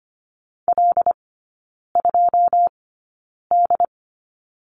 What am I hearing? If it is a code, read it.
Morse code, heard as a keyed tone, about 25 wpm, spaced as for 9 wpm, 690 Hz. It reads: L2D
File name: Lesson 14e - Number 2 - Three Characters 25wpm